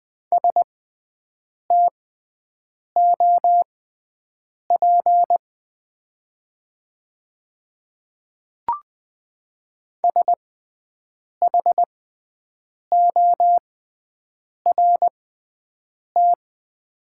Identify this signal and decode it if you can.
Morse code, heard as a keyed tone, about 20 wpm, spaced as for 7 wpm, 700 Hz. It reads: STOP SHORT
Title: 6x Character Spacing - Top 500 Words - 20wpm